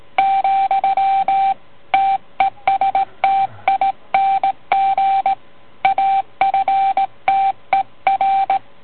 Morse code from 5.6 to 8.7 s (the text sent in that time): AFTER